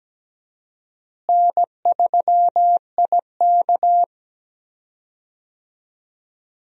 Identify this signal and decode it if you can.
Morse code, heard as a keyed tone, about 17 wpm, 700 Hz. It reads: N3IK
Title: USA Call Signs - 4 Characters - Repeated 3x - 17wpm